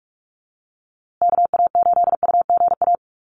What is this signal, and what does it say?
Morse code, heard as a keyed tone, about 46 wpm, 700 Hz. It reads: KA8UGA
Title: USA Call Signs Repeated 2x 45wpm